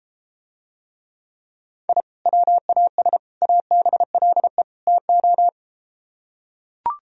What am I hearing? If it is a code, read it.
Morse code, heard as a keyed tone, about 33 wpm, 700 Hz. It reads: I WAS ABLE TO